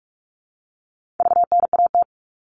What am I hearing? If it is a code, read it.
Morse code, heard as a keyed tone, about 44 wpm, 700 Hz. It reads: VNAT
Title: All Cut Numbers - 4-Digit 45wpm